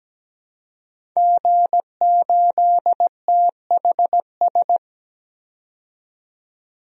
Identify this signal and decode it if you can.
Morse code, heard as a keyed tone, about 17 wpm, 700 Hz. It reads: G8THS
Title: DXCC-100 Call Signs - 5 Characters - 17wpm